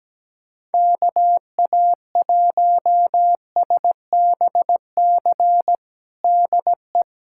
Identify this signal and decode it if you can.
Morse code, heard as a keyed tone, about 17 wpm, 700 Hz. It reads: KA1SBC DE
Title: USA Callsign DE Callsign - 6 Characters - 17wpm